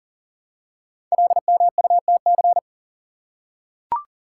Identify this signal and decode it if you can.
Morse code, heard as a keyed tone, about 38 wpm, 690 Hz. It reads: LMUTC
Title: Code Groups - 5 Letters - 40wpm